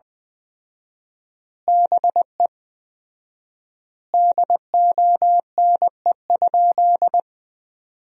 Morse code, heard as a keyed tone, about 19 wpm, 700 Hz. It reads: BE DONE?
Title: 4x Word-Spacing - Sentences from Top 500 Words - 20wpm